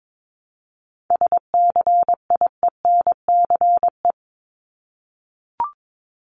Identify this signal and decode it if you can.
Morse code, heard as a keyed tone, about 22 wpm, 700 Hz. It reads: SCIENCE
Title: Top 500 Words 22wpm